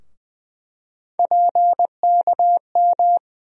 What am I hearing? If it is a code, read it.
Morse code, heard as a keyed tone, about 20 wpm, 700 Hz. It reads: PKM
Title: USA Call Sign Postfixes - 3 Letters 20wpm